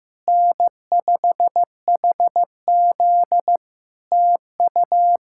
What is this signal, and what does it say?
Morse code, heard as a keyed tone, about 15 wpm, 700 Hz. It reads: N5HZ TU
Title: Practice QSO - Exchange 4 - Ending - 15wpm